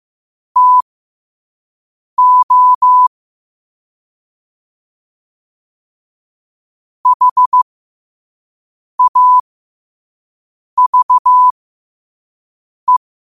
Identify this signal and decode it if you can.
Morse code, heard as a keyed tone, about 15 wpm, spaced as for 5 wpm, 1000 Hz. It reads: TO HAVE
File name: qotdfiles / 5wpm